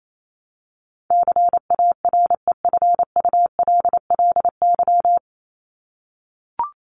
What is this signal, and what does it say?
Morse code, heard as a keyed tone, about 28 wpm, 700 Hz. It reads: CAREFULLY
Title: Top 1001-1250 Words 28wpm